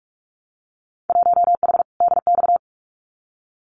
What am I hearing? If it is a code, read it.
Morse code, heard as a keyed tone, about 44 wpm, 700 Hz. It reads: 1H DX
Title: Winter Field Day Exchanges 45wpm